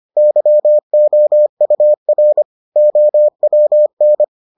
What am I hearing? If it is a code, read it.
Morse code, heard as a keyed tone, about 25 wpm, 600 Hz. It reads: YOUR OWN